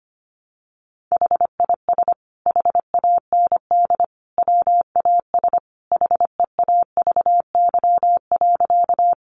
Morse code, heard as a keyed tone, about 25 wpm, 700 Hz. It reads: HIS HAND WAS HEAVY.
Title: Sentences from Top 1100 Words 25wpm